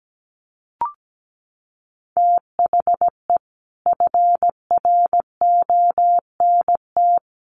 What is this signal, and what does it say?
Morse code, heard as a keyed tone, about 17 wpm, 700 Hz. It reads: THE FRONT
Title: Sets of 4 Words 17wpm